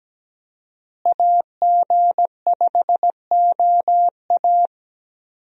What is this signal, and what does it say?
Morse code, heard as a keyed tone, about 17 wpm, 700 Hz. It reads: AG5OA
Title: USA Call Signs - 5 Characters - Repeated 2x - 17wpm